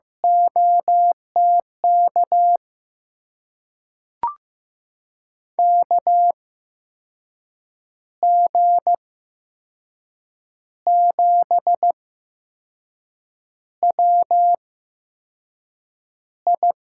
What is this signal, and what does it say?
Morse code, heard as a keyed tone, about 15 wpm, spaced as for 5 wpm, 700 Hz. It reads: OTK KG7WI
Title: USA Call Signs - 6 Characters - 8x Character Spacing - 15wpm